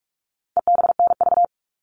Morse code, heard as a keyed tone, about 44 wpm, 700 Hz. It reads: EBNV
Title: All Cut Numbers - 4-Digit 45wpm